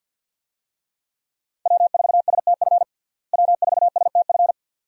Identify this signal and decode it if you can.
Morse code, heard as a keyed tone, about 51 wpm, 700 Hz. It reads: W4STF W4STF
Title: USA Call Signs - 5 Characters - Repeated 2x - 50wpm